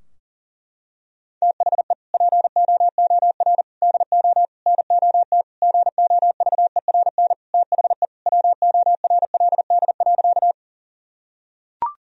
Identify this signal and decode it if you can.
Morse code, heard as a keyed tone, about 40 wpm, 700 Hz. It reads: THE POOR DO NOT GOVERN THE WORLD.
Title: Sentences from Top 1000 Words - Review Part 2 - 40wpm